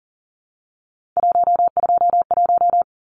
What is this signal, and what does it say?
Morse code, heard as a keyed tone, about 40 wpm, 700 Hz. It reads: 121
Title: Numbers - 3-Digit 40wpm